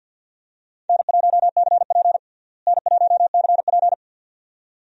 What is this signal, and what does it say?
Morse code, heard as a keyed tone, about 51 wpm, 700 Hz. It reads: N1CP N1CP
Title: USA Call Signs - 4 Characters - Repeated 2x - 50wpm